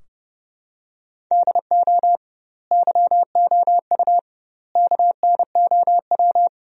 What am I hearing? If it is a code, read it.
Morse code, heard as a keyed tone, about 32 wpm, 700 Hz. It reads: DO YOU KNOW
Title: 2x Word-Spacing - Sentences from Top 100 Words - 30wpm